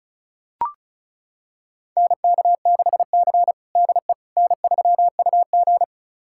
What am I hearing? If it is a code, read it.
Morse code, heard as a keyed tone, about 36 wpm, 700 Hz. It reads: NK6C DE N3UG
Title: USA Callsign DE Callsign - 4 Characters - 35wpm